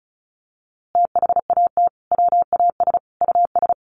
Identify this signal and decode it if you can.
Morse code, heard as a keyed tone, about 34 wpm, 700 Hz. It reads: THAT WAS US
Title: A Sound Of Thunder 35wpm